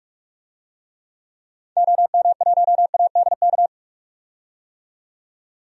Morse code, heard as a keyed tone, about 41 wpm, 700 Hz. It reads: OM1ADK
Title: DXCC-100 Call Signs - 6 Characters - 45wpm